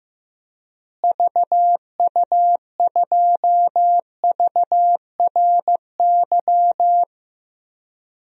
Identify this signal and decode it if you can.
Morse code, heard as a keyed tone, about 15 wpm, 700 Hz. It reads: VU2VRY
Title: DXCC-100 Call Signs - 6 Characters - Repeated 2x - 15wpm